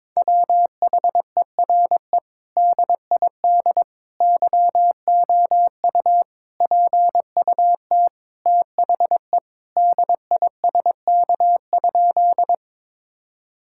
Morse code, heard as a keyed tone, about 22 wpm, 700 Hz. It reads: WHERE DID YOU PUT THE DISK?
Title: Sentences from Top 1000 Words - Review Part 6 - 22wpm